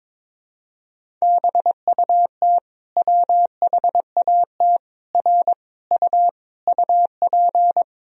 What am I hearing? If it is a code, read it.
Morse code, heard as a keyed tone, about 21 wpm, 700 Hz. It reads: BUT WHAT R U UP
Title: Sentences from Top 100 Words and Abbreviations 22wpm